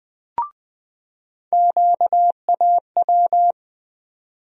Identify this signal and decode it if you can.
Morse code, heard as a keyed tone, about 20 wpm, 700 Hz. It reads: QAW
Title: USA Call Sign Postfixes - 3 Letters 20wpm